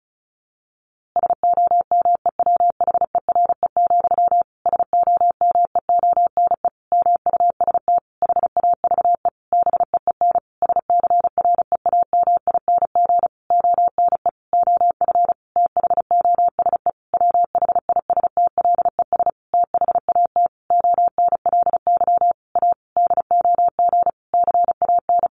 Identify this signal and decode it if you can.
Morse code, heard as a keyed tone, about 34 wpm, 700 Hz. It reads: SOMEWHERE, SOMEONE MUST HAVE BEEN SCREAMING ONE OF THOSE WHISTLES THAT ONLY A DOG CAN
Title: A Sound Of Thunder 35wpm